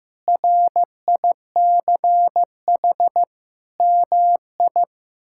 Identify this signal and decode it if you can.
Morse code, heard as a keyed tone, about 16 wpm, 700 Hz. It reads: RICH MI
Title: K1USN SST QSO Practice - 15wpm